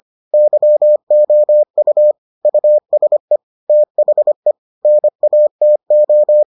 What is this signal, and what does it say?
Morse code, heard as a keyed tone, about 25 wpm, 600 Hz. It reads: YOU USE THE NATO